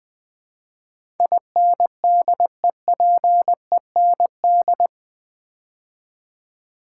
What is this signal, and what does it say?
Morse code, heard as a keyed tone, about 20 wpm, 700 Hz. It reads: INDEPEND
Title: Build A Word - 4 Letters At A Time - 11-Letter Words - 20wpm